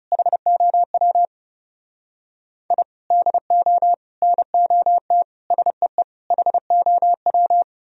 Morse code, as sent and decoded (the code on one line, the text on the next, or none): .... --- .-- / .. / -.. --- / -. --- - / ... . / .... --- .--
HOW I DO NOT SEE HOW